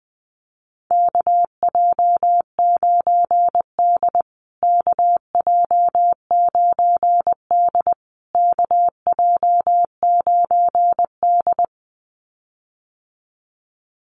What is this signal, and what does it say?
Morse code, heard as a keyed tone, about 20 wpm, 700 Hz. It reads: KJ9D KJ9D KJ9D
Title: USA Call Signs Repeated 3x 20wpm